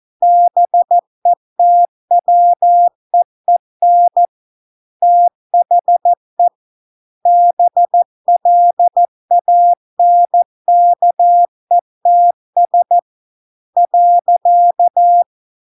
Morse code, heard as a keyed tone, about 14 wpm, 700 Hz. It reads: BETWEEN THE BLANKETS .